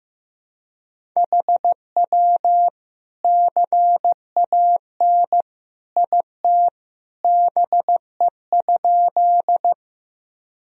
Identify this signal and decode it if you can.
Morse code, heard as a keyed tone, about 15 wpm, 700 Hz. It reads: HW CAN IT BE?
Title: Sentences from Top 100 Words and Abbreviations 15wpm